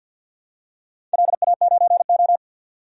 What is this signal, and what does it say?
Morse code, heard as a keyed tone, about 52 wpm, 700 Hz. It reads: LA9O
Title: DXCC-100 Call Signs - 4 Characters - 50wpm